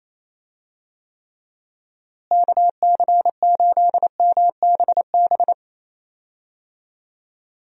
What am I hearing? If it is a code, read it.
Morse code, heard as a keyed tone, about 28 wpm, 700 Hz. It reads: KC8MBB
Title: USA Call Signs - 6 Characters - Repeated 2x - 28wpm